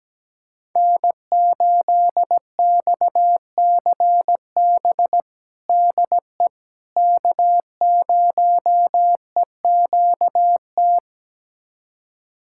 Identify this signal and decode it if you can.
Morse code, heard as a keyed tone, about 17 wpm, 700 Hz. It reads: N8XCB DE K0EQT
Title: USA Callsign DE Callsign - 5 Characters - 17wpm